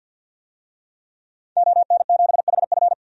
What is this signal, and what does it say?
Morse code, heard as a keyed tone, about 53 wpm, 700 Hz. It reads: ON7HF
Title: DXCC-100 Call Signs - 5 Characters - Repeated 3x - 50wpm